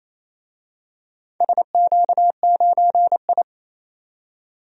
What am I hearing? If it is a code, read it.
Morse code, heard as a keyed tone, about 28 wpm, 700 Hz. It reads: SQ9I
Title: DXCC-100 Call Signs - 4 Characters - 28wpm